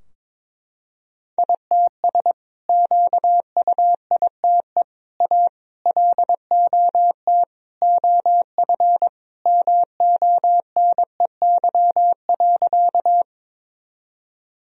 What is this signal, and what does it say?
Morse code, heard as a keyed tone, about 23 wpm, 700 Hz. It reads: ITS QUITE A LOT OF MONEY.
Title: Sentences from Top 1000 Words - Review Part 3 - 22wpm